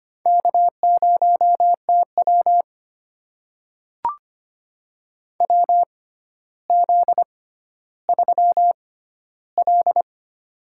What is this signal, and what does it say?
Morse code, heard as a keyed tone, about 25 wpm, spaced as for 13 wpm, 700 Hz. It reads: K0TW WZ3L
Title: USA Call Signs - 4 Characters - 6x Character Spacing - 25wpm